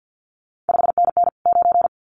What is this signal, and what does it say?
Morse code, heard as a keyed tone, about 50 wpm, 700 Hz. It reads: 5NN 9
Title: CQ WW DX Contest Exchange - 50wpm